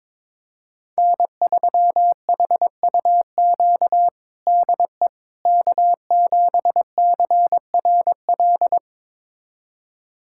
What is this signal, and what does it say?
Morse code, heard as a keyed tone, about 22 wpm, 700 Hz. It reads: N3HUQ DE K7CRL
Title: USA Callsign DE Callsign - 5 Characters - 22wpm